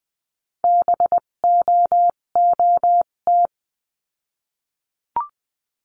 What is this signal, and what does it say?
Morse code, heard as a keyed tone, about 20 wpm, spaced as for 17 wpm, 700 Hz. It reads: BOOT
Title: 257 Lesson 42b - Words 14-20wpm